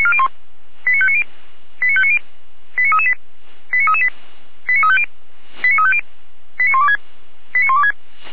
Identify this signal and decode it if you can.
Five-tone selective call (ZVEI), heard as quick runs of five tones, tones of about 80 ms. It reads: 89590 89590 89338 89338 89350 89350 89126 89126
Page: /455/SDR/ZVEI/